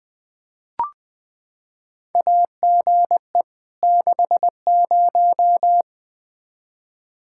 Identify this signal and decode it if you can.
Morse code, heard as a keyed tone, about 20 wpm, 700 Hz. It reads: AGE 60
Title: Practice QSO - Exchange 3 - More Info - 20wpm